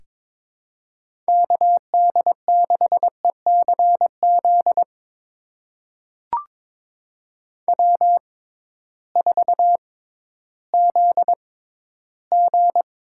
Morse code, heard as a keyed tone, about 21 wpm, spaced as for 13 wpm, 700 Hz. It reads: KD6ECZ W4ZG
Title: USA Call Signs - 6x Character Spacing - 22wpm